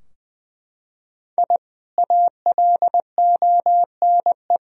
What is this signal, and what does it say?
Morse code, heard as a keyed tone, about 21 wpm, 700 Hz. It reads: I ALONE
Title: Sentences from Top 1000 Words - Review Part 2 - 20wpm